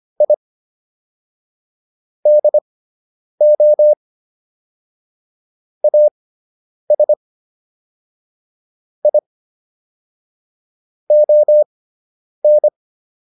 Morse code, heard as a keyed tone, about 25 wpm, spaced as for 9 wpm, 600 Hz. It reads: I DO AS I ON